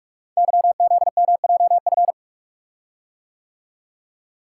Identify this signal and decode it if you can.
Morse code, heard as a keyed tone, about 44 wpm, 700 Hz. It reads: YZMJF
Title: Code Groups - 5 Letters - 45wpm